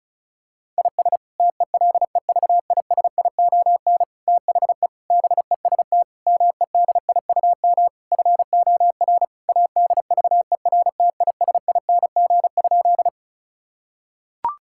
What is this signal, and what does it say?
Morse code, heard as a keyed tone, about 36 wpm, 700 Hz. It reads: IS TELEVISION THE BEST MEDIUM FOR ADVERTISING?
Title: Sentences from Top 1000 Words - Review Part 6 - 35wpm